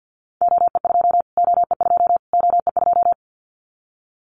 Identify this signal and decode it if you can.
Morse code, heard as a keyed tone, about 50 wpm, 700 Hz. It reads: OE2 OE2 OE2
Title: Callsign Prefixes for Top 25 DXCC - Repeated 3x - 50wpm